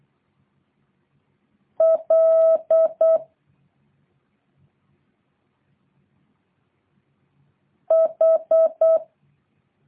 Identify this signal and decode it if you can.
Morse code, heard as a keyed tone, about 8 wpm, 650 Hz. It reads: L H